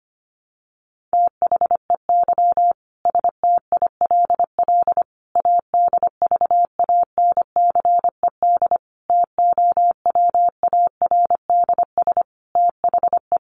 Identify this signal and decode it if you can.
Morse code, heard as a keyed tone, about 25 wpm, 700 Hz. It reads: THEY STILL ADVANCED TOWARDS THE
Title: Sentences from Top 1200 Words 25wpm